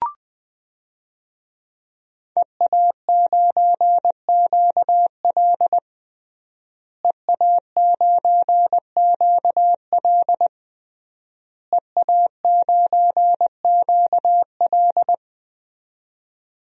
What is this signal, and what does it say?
Morse code, heard as a keyed tone, about 20 wpm, 700 Hz. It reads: EA9QL EA9QL EA9QL
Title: DXCC-100 Call Signs - 5 Characters - Repeated 3x - 20wpm